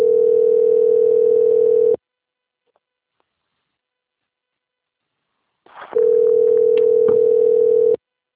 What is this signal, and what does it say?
Ringback tone (North America), 440 + 480 Hz.